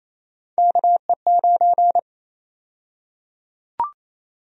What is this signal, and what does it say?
Morse code, heard as a keyed tone, about 28 wpm, 700 Hz. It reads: KE9
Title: USA Call Sign Prefixes - Repeated 2x - 28wpm